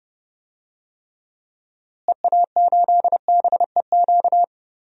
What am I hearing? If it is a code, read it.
Morse code, heard as a keyed tone, about 30 wpm, 700 Hz. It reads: EA8BEQ
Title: DXCC-100 Call Signs - 6 Characters - Repeated 2x - 30wpm